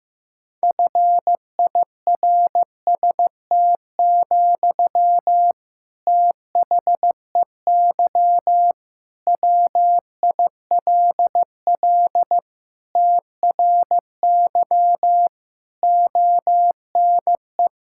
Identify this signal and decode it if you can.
Morse code, heard as a keyed tone, about 15 wpm, 700 Hz. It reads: FIRST, THEY WILL TRY ONE
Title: Sentences from Top 1000 Words - Review Part 6 - 15wpm